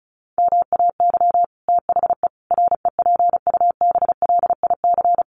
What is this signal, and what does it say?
Morse code, heard as a keyed tone, about 34 wpm, 700 Hz. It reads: MAY THE REPUBLIC
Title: Sentences from Top 1500 Words 35wpm